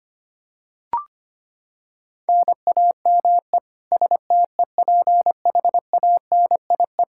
Morse code, heard as a keyed tone, about 25 wpm, 700 Hz. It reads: NAME STEPHANIE
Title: Practice QSO - Exchange 1 - Beginning - 25wpm